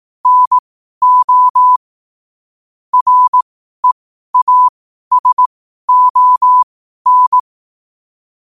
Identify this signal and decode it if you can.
Morse code, heard as a keyed tone, about 18 wpm, spaced as for 12 wpm, 1000 Hz. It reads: NO REASON
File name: qotdfiles / 12wpm